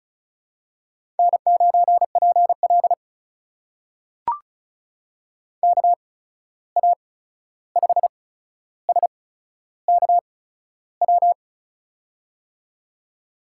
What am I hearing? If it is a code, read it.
Morse code, heard as a keyed tone, about 36 wpm, spaced as for 13 wpm, 700 Hz. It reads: N9PL KA5SKW
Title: USA Call Signs - 8x Character Spacing - 35wpm